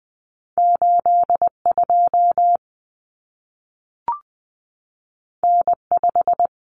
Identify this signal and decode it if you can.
Morse code, heard as a keyed tone, about 20 wpm, 700 Hz. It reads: 82 N5
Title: Common Cut Numbers - 2-Digit 20wpm